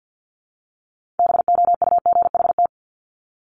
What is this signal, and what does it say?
Morse code, heard as a keyed tone, about 50 wpm, 690 Hz. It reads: BOUGHT
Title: Multi-letter Phonemes Repeated 3x 50wpm